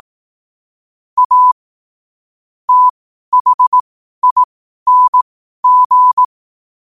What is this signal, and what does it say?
Morse code, heard as a keyed tone, about 18 wpm, spaced as for 12 wpm, 1000 Hz. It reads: A THING
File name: qotdfiles / 12wpm